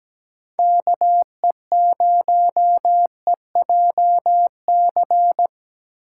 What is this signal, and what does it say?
Morse code, heard as a keyed tone, about 20 wpm, 700 Hz. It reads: KE0EJC
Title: Speed Racing - USA Call Signs 17wpm